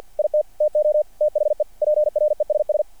Morse code, heard as a keyed tone, about 49 wpm, 600 Hz. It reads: IT TO THE PRESS
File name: archived / CRSnet